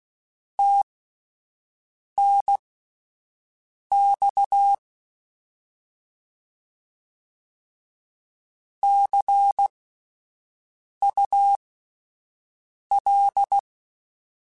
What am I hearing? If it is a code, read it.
Morse code, heard as a keyed tone, about 16 wpm, spaced as for 5 wpm, 800 Hz. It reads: TNX CUL
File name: Abreviations / Abreviations mp3s / Abreviations@5